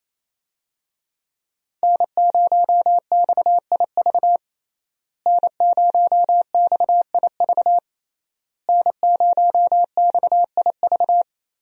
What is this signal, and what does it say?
Morse code, heard as a keyed tone, about 28 wpm, 700 Hz. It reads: N0XIV N0XIV N0XIV
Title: USA Call Signs - 5 Characters - Repeated 3x - 28wpm